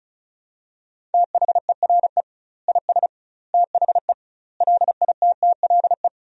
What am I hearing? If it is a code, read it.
Morse code, heard as a keyed tone, about 36 wpm, 700 Hz. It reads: THERE IS THE LITTLE
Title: 2x Word-Spacing - Sentences from Top 200 Words - 35wpm